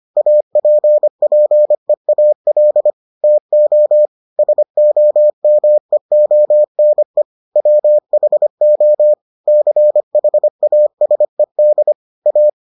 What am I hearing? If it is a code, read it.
Morse code, heard as a keyed tone, about 25 wpm, 600 Hz. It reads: APPEAL TO SOMEONE WHO CHASED A